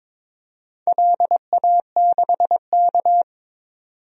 Morse code, heard as a keyed tone, about 22 wpm, 700 Hz. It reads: LA6K
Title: DXCC-100 Call Signs - 4 Characters - 22wpm